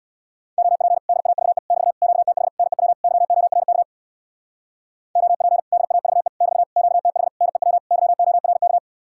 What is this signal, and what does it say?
Morse code, heard as a keyed tone, about 67 wpm, 700 Hz. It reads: OP DAVE <BT> QTH NEW YORK OP DAVE <BT> QTH NEW YORK